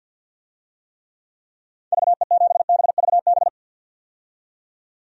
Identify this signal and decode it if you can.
Morse code, heard as a keyed tone, about 51 wpm, 700 Hz. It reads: VE7BVB
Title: DXCC-100 Call Signs - 6 Characters - 50wpm